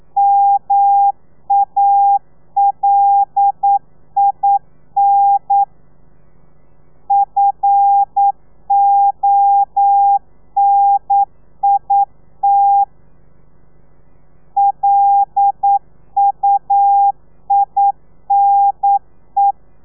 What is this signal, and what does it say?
Morse code, heard as a keyed tone, about 9 wpm, 800 Hz. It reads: MALIN FONIT LUINE
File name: telegrafia / i2tfj / Lez5 40